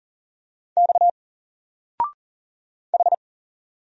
Morse code, heard as a keyed tone, about 40 wpm, 700 Hz. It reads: X H
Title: Single Letter-Number 40wpm